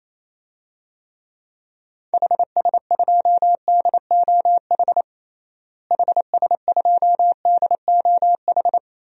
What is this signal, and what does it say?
Morse code, heard as a keyed tone, about 28 wpm, 700 Hz. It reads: HS2DOH HS2DOH
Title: DXCC-100 Call Signs - 6 Characters - Repeated 2x - 28wpm